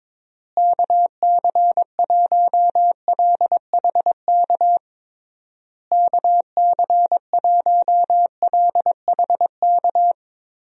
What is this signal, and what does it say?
Morse code, heard as a keyed tone, about 21 wpm, 700 Hz. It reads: KC1LHK KC1LHK
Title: USA Call Signs - 6 Characters - Repeated 3x - 22wpm